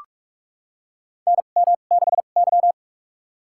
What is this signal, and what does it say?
Morse code, heard as a keyed tone, about 47 wpm, spaced as for 33 wpm, 700 Hz. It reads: NM6Y
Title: USA Call Signs - 4 Characters - 2x Character Spacing - 45wpm